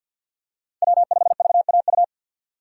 Morse code, heard as a keyed tone, about 47 wpm, 700 Hz. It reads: W5VAU
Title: USA Call Signs - 5 Characters - 4x Character Spacing - 50wpm